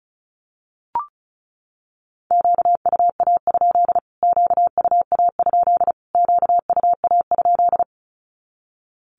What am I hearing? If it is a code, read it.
Morse code, heard as a keyed tone, about 34 wpm, 700 Hz. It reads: QUA? QUA? QUA?
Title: Q-Codes Repeated 3x 35wpm